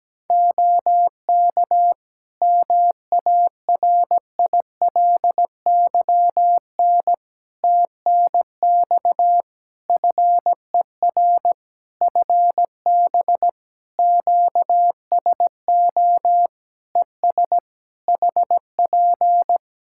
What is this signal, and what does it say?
Morse code, heard as a keyed tone, about 17 wpm, 700 Hz. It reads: OK MARILYN TNX FER FB QSO ES HP
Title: Practice QSO - Exchange 4 - Ending - 17wpm